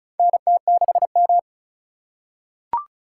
Morse code, heard as a keyed tone, about 36 wpm, 700 Hz. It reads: NT6M
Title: USA Call Signs - 2x Character Spacing - 35wpm